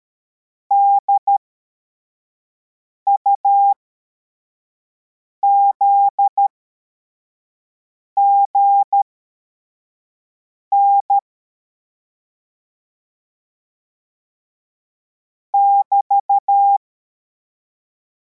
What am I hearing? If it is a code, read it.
Morse code, heard as a keyed tone, about 13 wpm, spaced as for 4 wpm, 800 Hz. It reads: DUZGN <BT>
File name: formation / cw / Text63